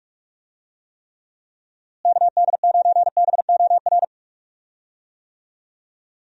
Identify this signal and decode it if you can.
Morse code, heard as a keyed tone, about 46 wpm, 700 Hz. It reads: KD9BOR
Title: USA Call Signs - 6 Characters - Repeated 3x - 45wpm